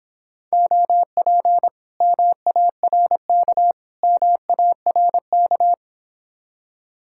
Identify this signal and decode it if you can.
Morse code, heard as a keyed tone, about 29 wpm, 700 Hz. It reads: OP MARK MARK